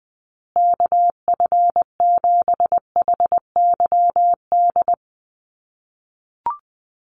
Morse code, heard as a keyed tone, about 20 wpm, 700 Hz. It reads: KF7HYD
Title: USA Call Signs Repeated 2x 20wpm